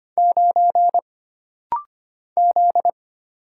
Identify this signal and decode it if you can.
Morse code, heard as a keyed tone, about 25 wpm, 700 Hz. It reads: 9 Z
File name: Single Character (Letter:Number:Punctuation:Prosign) - Rapid-Fire - 25wpm